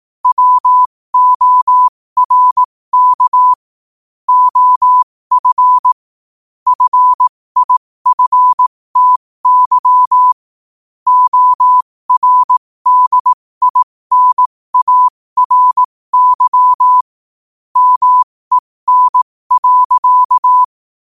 Morse code, heard as a keyed tone, about 18 wpm, spaced as for 15 wpm, 1000 Hz. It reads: WORK OF FIFTY ORDINARY MEN.
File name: qotdfiles / 15wpm